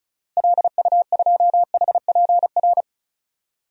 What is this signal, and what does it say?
Morse code, heard as a keyed tone, about 34 wpm, 700 Hz. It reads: LU2HPR
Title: DXCC-100 Call Signs - 6 Characters - Repeated 2x - 35wpm